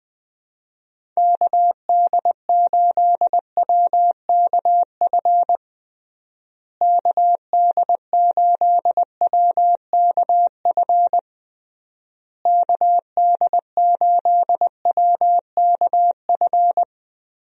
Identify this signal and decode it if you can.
Morse code, heard as a keyed tone, about 21 wpm, 700 Hz. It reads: KD8WKF KD8WKF KD8WKF
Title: USA Call Signs - 6 Characters - Repeated 3x - 20wpm